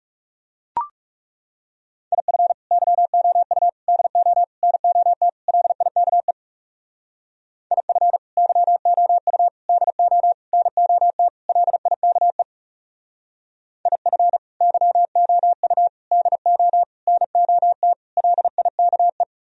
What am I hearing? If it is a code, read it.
Morse code, heard as a keyed tone, about 39 wpm, 700 Hz. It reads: IF YOU DO NOT LIKE IF YOU DO NOT LIKE IF YOU DO NOT LIKE